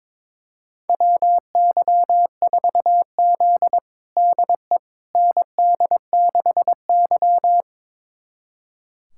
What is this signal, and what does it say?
Morse code, heard as a keyed tone, about 22 wpm, 700 Hz. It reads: WY4Z DE ND6Y
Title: USA Callsign DE Callsign - 4 Characters - 22wpm